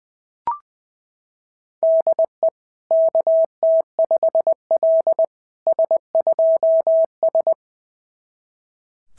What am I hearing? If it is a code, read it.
Morse code, heard as a keyed tone, about 20 wpm, 690 Hz. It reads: DE KT5L S2S
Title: SOTA QSO Practice 20wpm